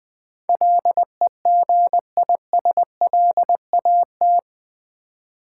Morse code, heard as a keyed tone, about 20 wpm, 700 Hz. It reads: LEGISLAT
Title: Build A Word - 4 Letters At A Time - 11-Letter Words - 20wpm